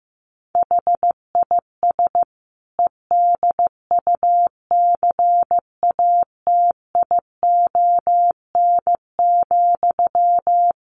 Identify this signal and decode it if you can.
Morse code, heard as a keyed tone, about 15 wpm, 700 Hz. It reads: HIS EDUCATION,
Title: Sentences from Top 1400 Words 15wpm